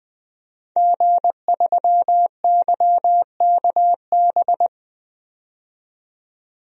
Morse code, heard as a keyed tone, about 20 wpm, 700 Hz. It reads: G3YKB
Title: DXCC-100 Call Signs - 5 Characters - 20wpm